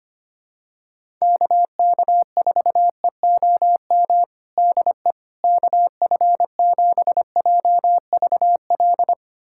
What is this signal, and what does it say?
Morse code, heard as a keyed tone, about 25 wpm, 700 Hz. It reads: KK4EOM DE KF7JVL
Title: USA Callsign DE Callsign - 6 Characters - 25wpm